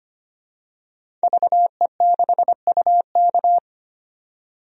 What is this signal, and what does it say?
Morse code, heard as a keyed tone, about 25 wpm, 700 Hz. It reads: VE6UK
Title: DXCC-100 Call Signs - 5 Characters - 25wpm